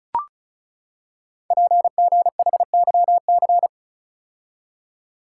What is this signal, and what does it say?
Morse code, heard as a keyed tone, about 36 wpm, 700 Hz. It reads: PGHYC
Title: Code Groups - 5 Letters - 35wpm